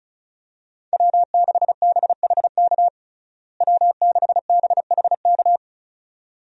Morse code, heard as a keyed tone, about 34 wpm, 700 Hz. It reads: W6BHK W6BHK
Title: USA Call Signs - 5 Characters - Repeated 2x - 35wpm